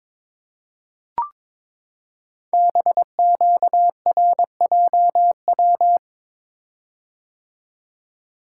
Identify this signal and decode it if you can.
Morse code, heard as a keyed tone, about 22 wpm, 700 Hz. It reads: BQRJW
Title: Code Groups - 5 Letters - 22wpm